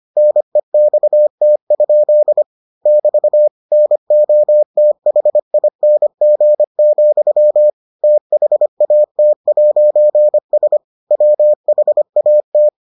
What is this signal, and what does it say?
Morse code, heard as a keyed tone, about 25 wpm, 600 Hz. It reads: NEXT? <BT> NOTHING, THAT'S WHAT